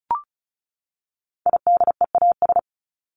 Morse code, heard as a keyed tone, about 34 wpm, 700 Hz. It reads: IDEAS
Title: ES:S Suffix - Top 100 Words with Base Words 35wpm